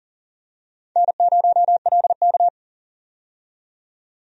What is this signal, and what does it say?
Morse code, heard as a keyed tone, about 40 wpm, 700 Hz. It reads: N0LK
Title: USA Call Signs - 4 Characters - 40wpm